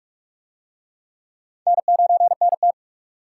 Morse code, heard as a keyed tone, about 43 wpm, 700 Hz. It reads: N9NT